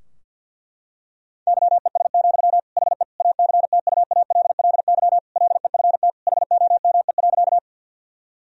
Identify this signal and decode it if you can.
Morse code, heard as a keyed tone, about 50 wpm, 700 Hz. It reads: YES, HE ACTUALLY LEFT HOME.
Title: Sentences from Top 1000 Words - Review Part 4 - 50wpm